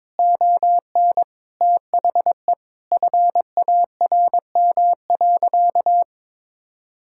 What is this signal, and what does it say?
Morse code, heard as a keyed tone, about 22 wpm, 700 Hz. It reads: ON THE FARM.
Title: Sentences from Top 1000 Words - Review Part 5 - 22wpm